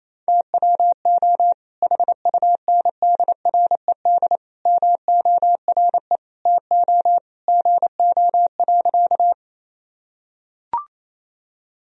Morse code, heard as a keyed tone, about 27 wpm, 700 Hz. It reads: TWO HUNDRED MORE TO GO.
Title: Sentences from Top 1000 Words - Review Part 2 - 28wpm